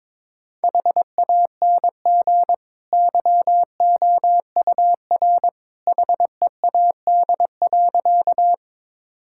Morse code, heard as a keyed tone, about 22 wpm, 700 Hz. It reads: HANG YOUR HEAD.
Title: Sentences from Top 1600 Words 22wpm